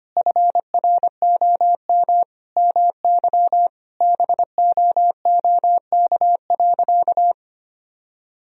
Morse code, heard as a keyed tone, about 25 wpm, 700 Hz. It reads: FROM MY BOOK.
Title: Sentences from Top 1000 Words - Review Part 1 - 25wpm